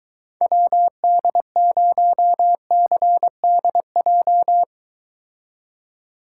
Morse code, heard as a keyed tone, about 23 wpm, 700 Hz. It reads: WD0CDJ